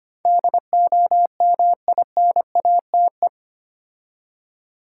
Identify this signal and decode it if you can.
Morse code, heard as a keyed tone, about 25 wpm, 700 Hz. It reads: DOMINATE
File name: 8 Letter Words - First 3 Letters Given - 25wpm